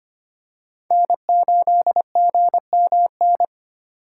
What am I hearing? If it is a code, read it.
Morse code, heard as a keyed tone, about 25 wpm, 700 Hz. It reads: N8GMN
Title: USA Call Signs - 2x Character Spacing - 25wpm